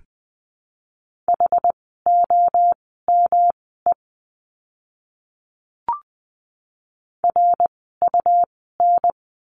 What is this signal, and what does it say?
Morse code, heard as a keyed tone, about 20 wpm, spaced as for 14 wpm, 700 Hz. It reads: HOME RUN
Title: 108 Lesson 18f - Letter m - Words 20wpm